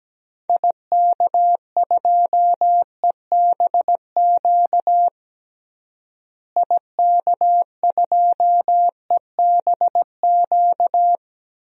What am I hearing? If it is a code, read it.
Morse code, heard as a keyed tone, about 17 wpm, 700 Hz. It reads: IK2EBQ IK2EBQ
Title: DXCC-100 Call Signs - 6 Characters - Repeated 3x - 17wpm